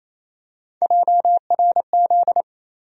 Morse code, heard as a keyed tone, about 28 wpm, 700 Hz. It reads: JRZ
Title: USA Call Sign Postfixes - 3 Letters 28wpm